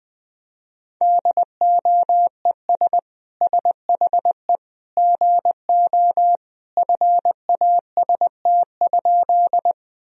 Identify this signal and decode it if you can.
Morse code, heard as a keyed tone, about 21 wpm, 700 Hz. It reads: DOES SHE GO FAST?
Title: Sentences from Top 1000 Words - Review Part 2 - 20wpm